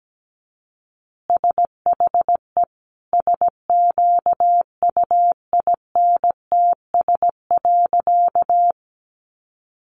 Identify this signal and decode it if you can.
Morse code, heard as a keyed tone, about 17 wpm, 700 Hz. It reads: SHE SQUINTS.
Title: Additional Easy Phrases 17wpm